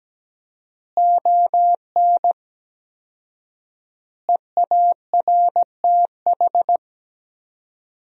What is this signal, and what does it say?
Morse code, heard as a keyed tone, about 17 wpm, 700 Hz. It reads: ON EARTH
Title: 4x Word-Spacing - Sentences from Top 200 Words - 17wpm